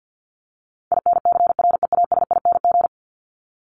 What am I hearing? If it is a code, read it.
Morse code, heard as a keyed tone, about 49 wpm, 700 Hz. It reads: INCREASING
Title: ING Suffix - Top 100 Words with Base Words 50wpm